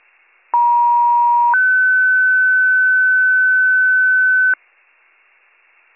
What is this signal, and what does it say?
Two-tone page, heard as a one-second tone, then a three-second tone, about 950 then 1500 Hz, with noise. It is heard from 0.5 to 4.5 s.